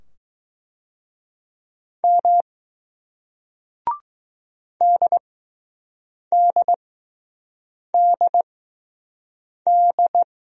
Morse code, heard as a keyed tone, about 20 wpm, spaced as for 6 wpm, 700 Hz. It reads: M DDDD